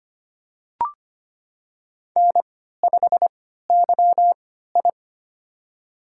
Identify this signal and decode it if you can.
Morse code, heard as a keyed tone, about 25 wpm, spaced as for 14 wpm, 700 Hz. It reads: N5YI
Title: USA Call Signs - 4 Characters - 3x Character Spacing - 25wpm